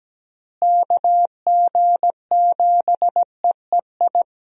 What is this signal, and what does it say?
Morse code, heard as a keyed tone, about 17 wpm, 700 Hz. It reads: KG7EEI
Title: USA Call Signs - 6 Characters - Repeated 2x - 17wpm